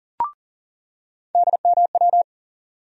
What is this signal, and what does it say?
Morse code, heard as a keyed tone, about 40 wpm, 700 Hz. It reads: DMW
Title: USA Call Sign Postfixes - 3 Letters 40wpm